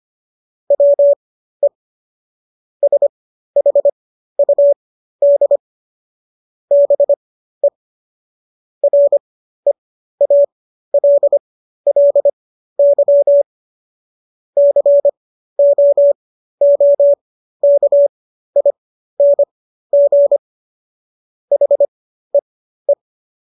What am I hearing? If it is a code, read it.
Morse code, heard as a keyed tone, about 25 wpm, spaced as for 13 wpm, 570 Hz. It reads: WE SHUD BE REALLY COOKING HEE